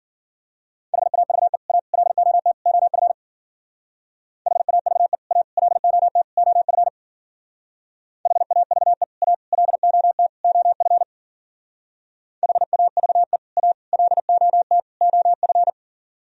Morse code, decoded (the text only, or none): HAVE A LOT OF HAVE A LOT OF HAVE A LOT OF HAVE A LOT OF